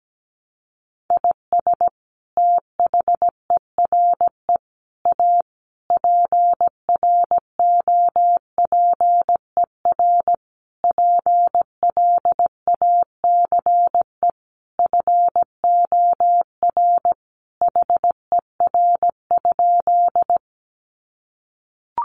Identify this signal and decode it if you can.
Morse code, heard as a keyed tone, about 17 wpm, 700 Hz. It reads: IS THERE A PROPER PLACE FOR HER?
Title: Sentences from Top 800 Words 17wpm